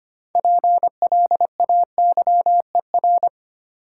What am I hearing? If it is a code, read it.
Morse code, heard as a keyed tone, about 25 wpm, 700 Hz. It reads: PLAYER
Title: Sentences from Top 1000 Words - Review Part 4 - New Word Drill - 25wpm